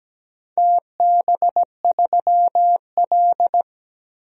Morse code, heard as a keyed tone, about 17 wpm, 700 Hz. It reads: TB3L
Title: DXCC-100 Call Signs - 4 Characters - Repeated 3x - 17wpm